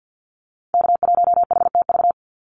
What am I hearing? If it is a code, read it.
Morse code, heard as a keyed tone, about 50 wpm, 700 Hz. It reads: K1HTV
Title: Call Signs - 50wpm